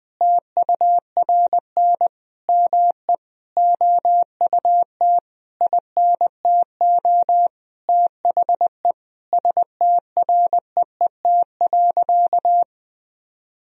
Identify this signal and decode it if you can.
Morse code, heard as a keyed tone, about 21 wpm, 700 Hz. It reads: TURN ME OUT INTO THE STREET.
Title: Sentences from Top 1000 Words - Review Part 4 - 20wpm